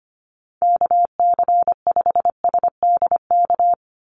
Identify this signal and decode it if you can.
Morse code, heard as a keyed tone, about 25 wpm, 690 Hz. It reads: KC5SDK
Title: USA Call Signs 25wpm